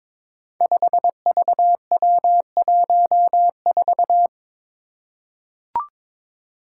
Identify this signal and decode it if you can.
Morse code, heard as a keyed tone, about 23 wpm, 700 Hz. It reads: 5VW14
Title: Five Challenging Letter-Number Combinations 22wpm